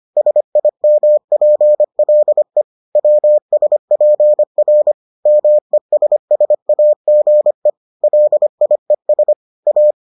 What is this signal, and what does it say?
Morse code, heard as a keyed tone, about 25 wpm, 600 Hz. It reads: SIMPLE WSPR MESSAGE LIES A